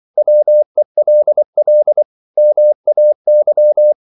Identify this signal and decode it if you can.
Morse code, heard as a keyed tone, about 24 wpm, 600 Hz. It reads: WELL MAY
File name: CW / Practice / 100MostCommonEnglishWords24